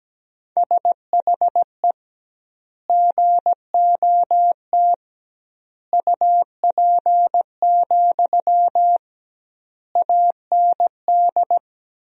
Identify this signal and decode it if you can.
Morse code, heard as a keyed tone, about 17 wpm, 700 Hz. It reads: SHE GOT UP, AND
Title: 2x Word-Spacing - Sentences from Top 500 Words - 17wpm